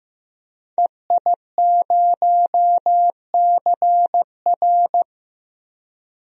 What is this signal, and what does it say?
Morse code, heard as a keyed tone, about 15 wpm, 700 Hz. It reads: EI0CR
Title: DXCC-100 Call Signs - 5 Characters - Repeated 2x - 15wpm